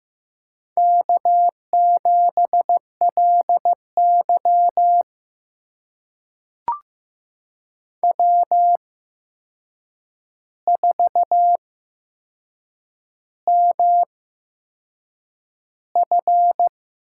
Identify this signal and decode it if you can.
Morse code, heard as a keyed tone, about 15 wpm, spaced as for 6 wpm, 700 Hz. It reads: K7LY W4MF
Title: USA Call Signs - 4 Characters - 8x Character Spacing - 15wpm